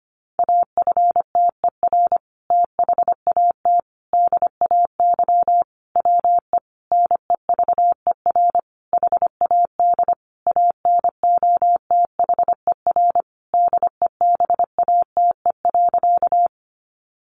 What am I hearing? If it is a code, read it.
Morse code, heard as a keyed tone, about 25 wpm, 700 Hz. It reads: AFTER THAT DAY WE NEVER HAD ANOTHER DEBATE.
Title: Sentences from Top 1100 Words 25wpm